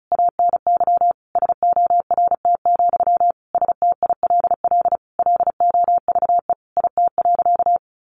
Morse code, heard as a keyed tone, about 34 wpm, 700 Hz. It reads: ANY SORT, STILL LOVE IT.
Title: Sentences from Top 1500 Words 35wpm